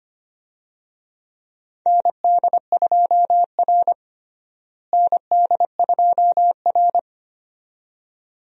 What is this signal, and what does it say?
Morse code, heard as a keyed tone, about 25 wpm, 700 Hz. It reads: ND2R ND2R
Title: USA Call Signs - 4 Characters - Repeated 2x - 25wpm